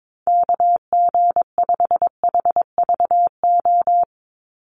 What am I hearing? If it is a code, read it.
Morse code, heard as a keyed tone, about 22 wpm, 700 Hz. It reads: KG5HVO
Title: Call Signs - 22wpm